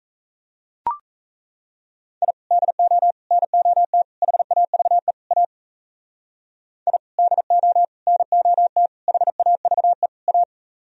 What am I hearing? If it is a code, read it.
Morse code, heard as a keyed tone, about 37 wpm, 700 Hz. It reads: I DO NOT HAVE A I DO NOT HAVE A